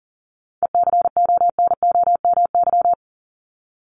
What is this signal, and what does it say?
Morse code, heard as a keyed tone, about 40 wpm, 700 Hz. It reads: ECONOMY
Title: Top 501-750 Words 40wpm